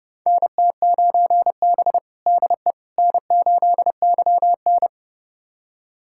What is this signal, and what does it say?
Morse code, heard as a keyed tone, about 29 wpm, 700 Hz. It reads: NT9B DE N8YN
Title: USA Callsign DE Callsign - 4 Characters - 30wpm